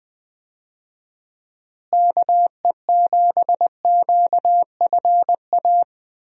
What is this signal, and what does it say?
Morse code, heard as a keyed tone, about 21 wpm, 700 Hz. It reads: KE7QFA
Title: USA Call Signs - 6 Characters - Repeated 2x - 20wpm